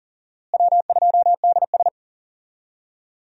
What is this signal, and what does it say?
Morse code, heard as a keyed tone, about 40 wpm, 700 Hz. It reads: W2DS
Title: USA Call Signs - 4 Characters - Repeated 3x - 40wpm